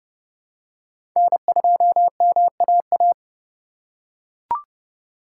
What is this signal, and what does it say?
Morse code, heard as a keyed tone, about 30 wpm, 700 Hz. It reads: N2MAA
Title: USA Call Signs - 5 Characters - 4x Character Spacing - 30wpm